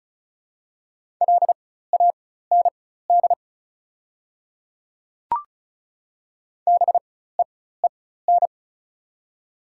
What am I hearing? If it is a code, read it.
Morse code, heard as a keyed tone, about 36 wpm, spaced as for 16 wpm, 700 Hz. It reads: LAND BEEN
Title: 4x Character Spacing - Top 200 Words - 35wpm